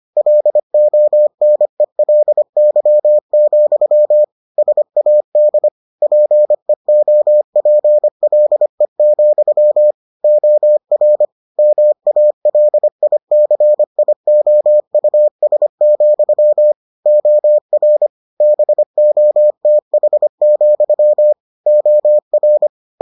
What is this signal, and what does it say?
Morse code, heard as a keyed tone, about 25 wpm, 600 Hz. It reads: LONELY, SAD PEOPLE, OR MALICIOUS, OR BOTH, OR